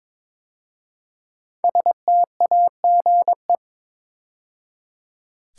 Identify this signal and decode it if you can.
Morse code, heard as a keyed tone, about 22 wpm, 700 Hz. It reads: STAGE
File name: 5 Letter Words - First 3 Letters Given - 22wpm